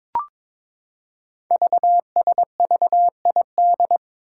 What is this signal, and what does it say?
Morse code, heard as a keyed tone, about 22 wpm, 700 Hz. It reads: VSVID
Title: Code Groups - 5 Letters - 22wpm